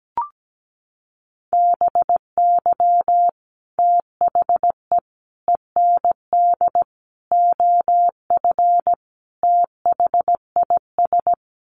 Morse code, heard as a keyed tone, about 17 wpm, 700 Hz. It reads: BY THE END OF THIS
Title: Sets of 5 Words Encore 17wpm